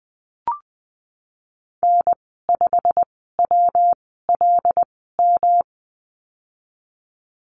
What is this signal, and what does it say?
Morse code, heard as a keyed tone, about 19 wpm, spaced as for 14 wpm, 700 Hz. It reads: N5WLM
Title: 238 Lesson 38g - Letter q - Call Signs 20wpm